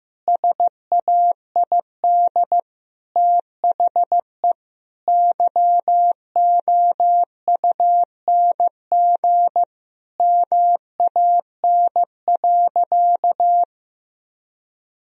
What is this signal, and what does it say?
Morse code, heard as a keyed tone, about 15 wpm, 700 Hz. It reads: SAID THE YOUNG MAN.
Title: Sentences from Top 1000 Words - Review Part 2 - 15wpm